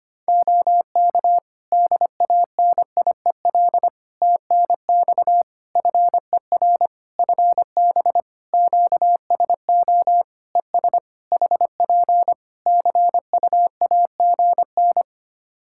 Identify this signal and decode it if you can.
Morse code, heard as a keyed tone, about 25 wpm, 700 Hz. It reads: OK DANIEL TNX FER FB QSO ES HP CUAGN
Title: Practice QSO - Exchange 4 - Ending - 25wpm